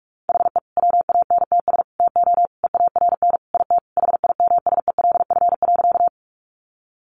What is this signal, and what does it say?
Morse code, heard as a keyed tone, about 44 wpm, 700 Hz. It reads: HE WANTS TO EARN IT HIMSELF.
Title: Additional Easy Phrases 45wpm